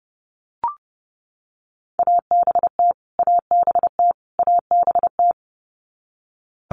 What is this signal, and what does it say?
Morse code, heard as a keyed tone, about 30 wpm, 700 Hz. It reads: ABT ABT ABT
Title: Abbreviations Repeated 3x - 30wpm